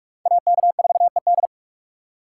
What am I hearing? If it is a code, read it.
Morse code, heard as a keyed tone, about 56 wpm, 700 Hz. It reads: AK4ED
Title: Speed Racing - USA 2x2 Call Signs 45wpm